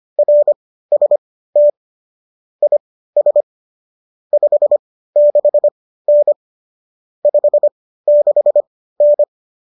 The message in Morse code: RST IS 56N 56N

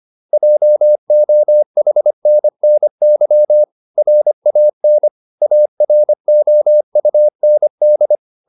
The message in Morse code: JOHNNY RAN AROUND

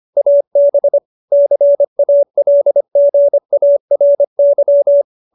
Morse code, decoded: AB CALGARY

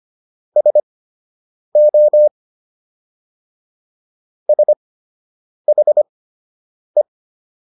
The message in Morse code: SO SHE

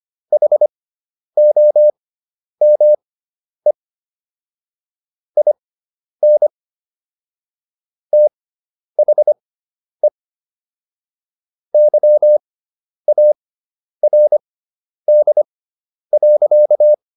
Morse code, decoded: HOME IN THE YARD.

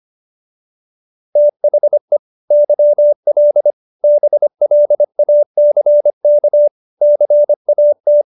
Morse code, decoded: THE YL BLACK CAT

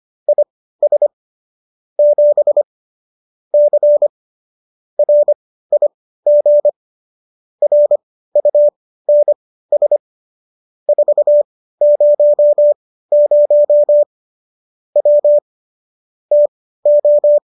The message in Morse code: IS 7 C RIG RUNS 400 W TO

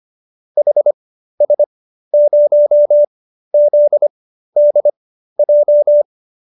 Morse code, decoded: HS0ZDJ

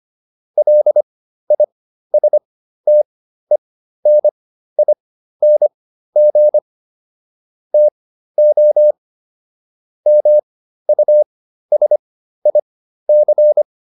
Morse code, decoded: LISTENING TO MUSIC

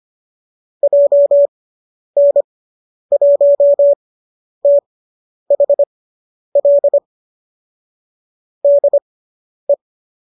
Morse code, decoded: JN1THL DE

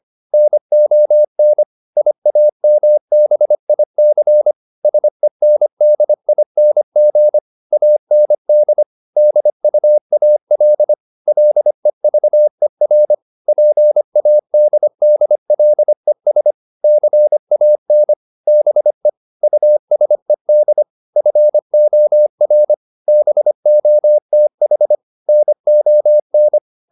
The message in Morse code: NON IAMBIC SENDING AND DUAL LEVER PADDLES CAN BE USED FOR BOTH NON

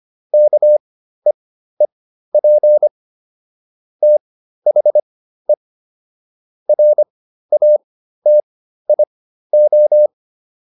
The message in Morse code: KEEP THE RATIO

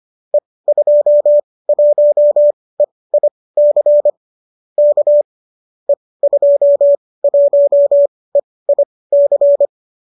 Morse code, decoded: E21EIC K E21EIC

